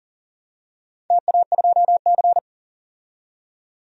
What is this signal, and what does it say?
Morse code, heard as a keyed tone, about 40 wpm, 700 Hz. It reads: TA2C
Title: DXCC-100 Call Signs - 4 Characters - 40wpm